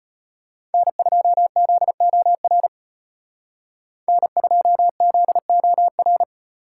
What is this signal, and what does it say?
Morse code, heard as a keyed tone, about 31 wpm, 700 Hz. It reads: N2ZOR N2ZOR